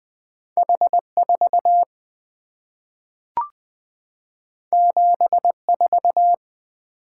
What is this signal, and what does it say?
Morse code, heard as a keyed tone, about 20 wpm, 700 Hz. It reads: H4 74
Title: Two Letter-Number Combinations - 20wpm